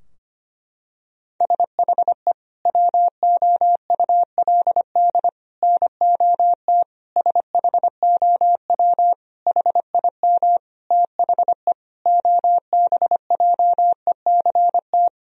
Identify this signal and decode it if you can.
Morse code, heard as a keyed tone, about 25 wpm, 700 Hz. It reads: SHE WOULD NOT SHOW HIM THE OBJECT
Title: Sentences from Top 1000 Words - Review Part 3 - 25wpm